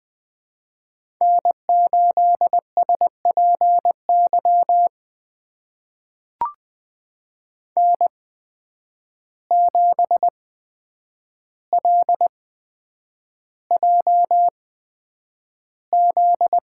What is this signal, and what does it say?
Morse code, heard as a keyed tone, about 20 wpm, spaced as for 9 wpm, 700 Hz. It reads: N8SPY N7LJZ
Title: USA Call Signs - 5 Characters - 8x Character Spacing - 20wpm